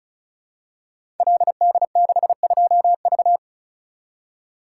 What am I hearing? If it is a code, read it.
Morse code, heard as a keyed tone, about 36 wpm, 700 Hz. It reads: LD62V
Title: Code Groups - 5 Letter:Numbers - 35wpm